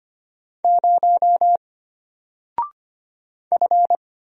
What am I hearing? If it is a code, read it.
Morse code, heard as a keyed tone, about 25 wpm, 700 Hz. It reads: 0 F
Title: Single Character (Letter:Number:Punctuation:Prosign) Alt 25wpm